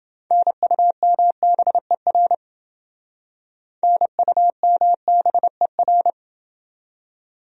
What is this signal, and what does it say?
Morse code, heard as a keyed tone, about 27 wpm, 700 Hz. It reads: NUMBER NUMBER